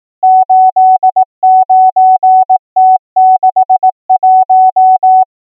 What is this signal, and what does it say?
Morse code, heard as a keyed tone, about 18 wpm, 750 Hz. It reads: 89T61
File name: numbers / V2-morse